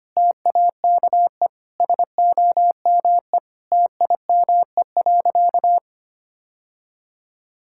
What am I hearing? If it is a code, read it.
Morse code, heard as a keyed tone, about 25 wpm, 700 Hz. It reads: TAKE SOME TIME.
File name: practice / 25wpm / Sentences with 3-Word Phrases - 25wpm